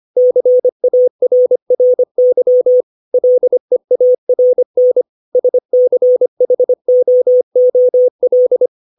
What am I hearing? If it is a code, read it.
Morse code, heard as a keyed tone, about 25 wpm, 500 Hz. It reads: CARRY LEARN SCHOOL
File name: 632 words / 632 Words 25WPM